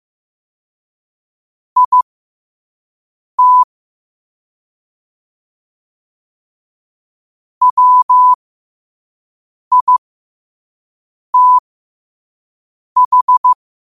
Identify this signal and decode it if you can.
Morse code, heard as a keyed tone, about 15 wpm, spaced as for 5 wpm, 1000 Hz. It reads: IT WITH
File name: qotdfiles / 5wpm